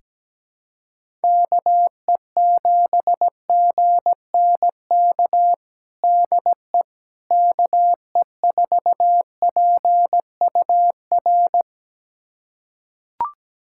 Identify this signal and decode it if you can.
Morse code, heard as a keyed tone, about 17 wpm, 700 Hz. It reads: KE7GNK DE KE4PUR
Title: USA Callsign DE Callsign - 6 Characters - 17wpm